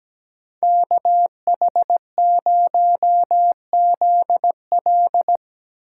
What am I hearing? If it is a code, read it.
Morse code, heard as a keyed tone, about 17 wpm, 700 Hz. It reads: KH0ZL
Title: USA Call Signs - 5 Characters - 2x Character Spacing - 17wpm